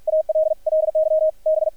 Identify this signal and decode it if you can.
Morse code, heard as a keyed tone, about 50 wpm, 630 Hz, with radio noise. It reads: ALLY B
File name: archived / CRSnet